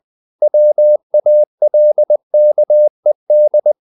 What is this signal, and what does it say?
Morse code, heard as a keyed tone, about 20 wpm, 600 Hz. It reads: WALKED
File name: the old man and the sea / the old man and the sea part 1 20wpm